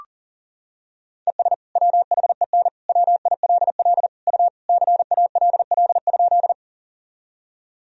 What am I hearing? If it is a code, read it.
Morse code, heard as a keyed tone, about 40 wpm, 700 Hz. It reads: ES WHEN WILL U CALL?
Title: Sentences from Top 100 Words and Abbreviations 40wpm